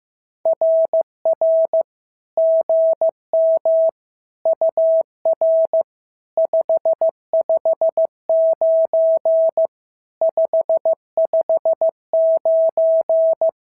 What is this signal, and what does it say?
Morse code, heard as a keyed tone, about 14 wpm, 650 Hz. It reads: RR GM UR 559 559
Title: SOTA QSO Practice 15wpm